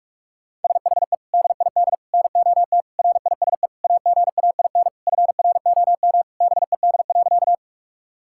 Morse code, heard as a keyed tone, about 46 wpm, 690 Hz. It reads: SHE DID NOT RISE AGAIN FROM BED.
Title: Sentences from Top 1000 Words - Review Part 6 - 45wpm